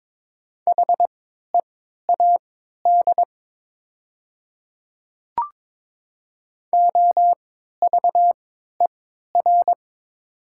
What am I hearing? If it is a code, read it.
Morse code, heard as a keyed tone, about 22 wpm, spaced as for 12 wpm, 700 Hz. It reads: HEAD OVER
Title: 3x Character Spacing - Top 500 Words - 22wpm